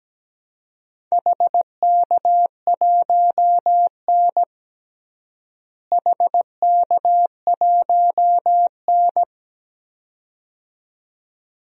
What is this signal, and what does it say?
Morse code, heard as a keyed tone, about 17 wpm, 700 Hz. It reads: HK1N HK1N
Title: DXCC-100 Call Signs - 4 Characters - Repeated 2x - 17wpm